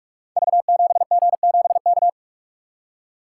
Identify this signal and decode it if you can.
Morse code, heard as a keyed tone, about 44 wpm, 700 Hz. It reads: V7G7K
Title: Code Groups - 5 Letter:Numbers - 45wpm